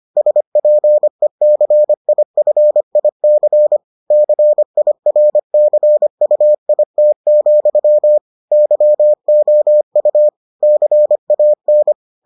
Morse code, heard as a keyed tone, about 25 wpm, 600 Hz. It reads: SPECIFIC CIRCUIT, YOU CAN